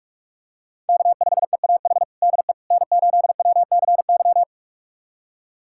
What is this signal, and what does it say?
Morse code, heard as a keyed tone, about 41 wpm, 700 Hz. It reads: K5EAH DE N8WCY
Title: USA Callsign DE Callsign - 5 Characters - 45wpm